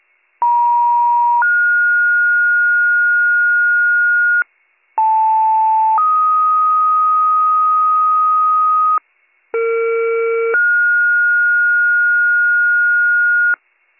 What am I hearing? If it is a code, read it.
Two-tone page, heard as a one-second tone, then a three-second tone, about 950 then 1430 Hz, about 880 then 1250 Hz, about 460 then 1430 Hz, with noise. It is heard from 0.4 to 13.6 s.